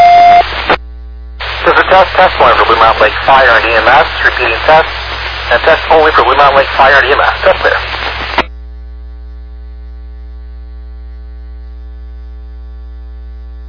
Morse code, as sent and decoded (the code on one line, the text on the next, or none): none